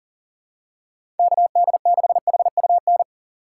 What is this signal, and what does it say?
Morse code, heard as a keyed tone, about 37 wpm, 700 Hz. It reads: KD6HUN
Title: USA Call Signs - 6 Characters - 3x Character Spacing - 40wpm